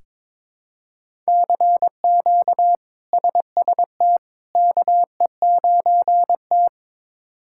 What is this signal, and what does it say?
Morse code, heard as a keyed tone, about 22 wpm, 700 Hz. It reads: CQ SST KE9T
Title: K1USN SST QSO Practice - 22wpm